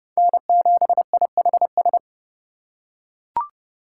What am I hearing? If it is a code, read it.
Morse code, heard as a keyed tone, about 28 wpm, 700 Hz. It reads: N7IHS